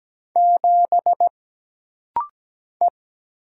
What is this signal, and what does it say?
Morse code, heard as a keyed tone, about 17 wpm, 700 Hz. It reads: 7 E
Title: Single Letter-Number - Rapid-Fire 17wpm